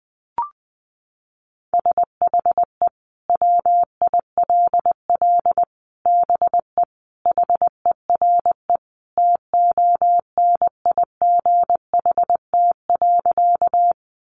Morse code, heard as a keyed tone, about 20 wpm, 700 Hz. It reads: SHE WILL BE HERE TONIGHT.
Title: Sentences from Top 600 Words 20wpm